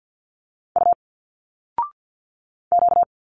Morse code, heard as a keyed tone, about 50 wpm, 700 Hz. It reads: U Q
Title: Single Letters 50wpm